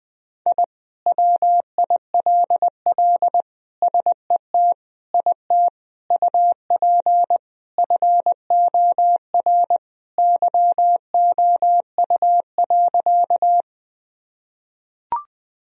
Morse code, heard as a keyed tone, about 20 wpm, 700 Hz. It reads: I WILL SET IT UP FOR YOU.
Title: Sentences from Top 1000 Words - Review Part 1 - 20wpm